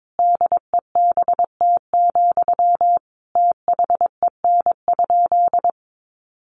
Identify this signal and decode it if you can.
Morse code, heard as a keyed tone, about 22 wpm, 700 Hz. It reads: DEBT, THEN?
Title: Sentences from Top 1300 Words 22wpm